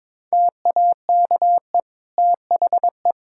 Morse code, heard as a keyed tone, about 22 wpm, 700 Hz. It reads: TAKE THE
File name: Difficult Sentences with Hints - 22wpm